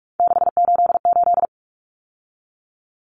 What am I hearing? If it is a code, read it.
Morse code, heard as a keyed tone, about 44 wpm, 700 Hz. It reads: 688
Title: Numbers - 3-Digit 45wpm